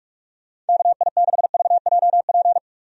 Morse code, heard as a keyed tone, about 42 wpm, 700 Hz. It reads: KI6VJP
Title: USA Call Signs - 6 Characters - 45wpm